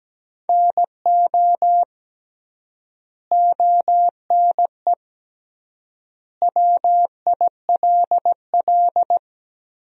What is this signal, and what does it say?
Morse code, heard as a keyed tone, about 17 wpm, 700 Hz. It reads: NO ONE WILL
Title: 3x Word-Spacing - Sentences from Top 100 Words - 17wpm